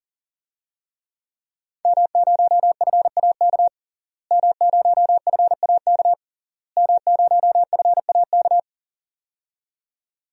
Morse code, heard as a keyed tone, about 40 wpm, 700 Hz. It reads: M0FAK M0FAK M0FAK
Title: DXCC-100 Call Signs - 5 Characters - Repeated 3x - 40wpm